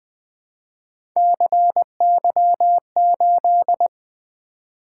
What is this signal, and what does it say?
Morse code, heard as a keyed tone, about 20 wpm, 700 Hz. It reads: CY8
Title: Three Letter-Number Combinations - 20wpm